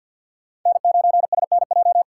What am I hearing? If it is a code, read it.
Morse code, heard as a keyed tone, about 52 wpm, 700 Hz. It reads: N9SNJ
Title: USA Call Signs - 5 Characters - 3x Character Spacing - 50wpm